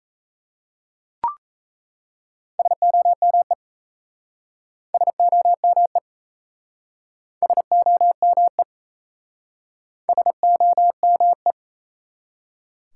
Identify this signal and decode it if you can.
Morse code, heard as a keyed tone, about 38 wpm, 700 Hz. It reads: SOME SOME SOME SOME